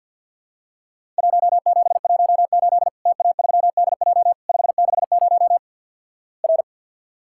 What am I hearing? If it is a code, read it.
Morse code, heard as a keyed tone, about 50 wpm, 700 Hz. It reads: 1718 TA3DJ 560 R
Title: Worked All Europe DX Contest - QTC Exchange - 50wpm